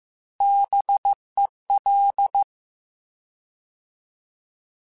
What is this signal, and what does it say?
Morse code, heard as a keyed tone, about 15 wpm, 800 Hz. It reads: BEL
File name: Alphabetical / Alphabetical mp3s / Bs @13